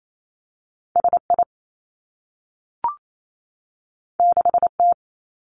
Morse code, heard as a keyed tone, about 28 wpm, 700 Hz. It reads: S I 6 T
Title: Two Challenging Letter-Number Combinations 28wpm